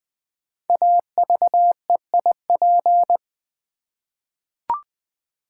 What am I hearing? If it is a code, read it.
Morse code, heard as a keyed tone, about 20 wpm, 700 Hz. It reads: AVEIP
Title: Code Groups - 5 Letters - 20wpm